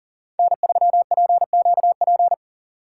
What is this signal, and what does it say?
Morse code, heard as a keyed tone, about 53 wpm, 700 Hz. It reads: N3PQP